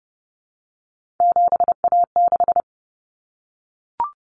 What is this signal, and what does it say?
Morse code, heard as a keyed tone, about 30 wpm, 700 Hz. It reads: 7A6
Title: Common Cut Numbers - 3-Digit 30wpm